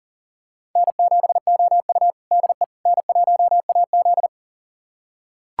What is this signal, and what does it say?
Morse code, heard as a keyed tone, about 37 wpm, 700 Hz. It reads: N7OU DE N1AZ
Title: USA Callsign DE Callsign - 4 Characters - 40wpm